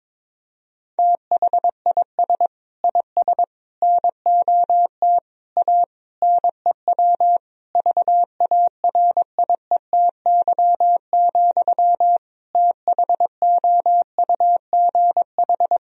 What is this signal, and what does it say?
Morse code, heard as a keyed tone, about 22 wpm, 700 Hz. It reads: THIS IS NOT A NEW VARIETY, THOUGH
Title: Sentences from Top 1000 Words - Review Part 4 - 22wpm